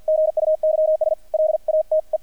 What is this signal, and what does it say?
Morse code, heard as a keyed tone, about 50 wpm, 630 Hz, with radio noise. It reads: GUYS LATE
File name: archived / CRSnet